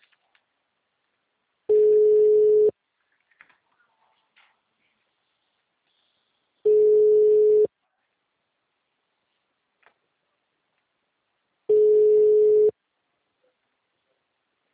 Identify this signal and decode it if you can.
Ringback tone (Europe, 425 Hz).
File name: out- -2013- - - .26364